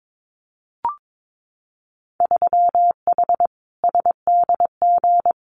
Morse code, heard as a keyed tone, about 22 wpm, 700 Hz. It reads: 3H SDG
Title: Winter Field Day Exchanges Encore 22wpm